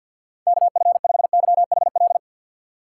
Morse code, heard as a keyed tone, about 52 wpm, 690 Hz. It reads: KF5YHL
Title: USA Call Signs - 6 Characters - 3x Character Spacing - 50wpm